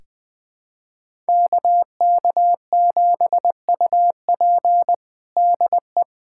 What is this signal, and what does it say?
Morse code, heard as a keyed tone, about 20 wpm, 700 Hz. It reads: KK7UP DE
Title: USA Callsign DE Callsign - 5 Characters - 20wpm